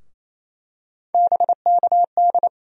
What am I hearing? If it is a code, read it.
Morse code, heard as a keyed tone, about 28 wpm, 700 Hz. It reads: BKD
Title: USA Call Sign Postfixes - 3 Letters 28wpm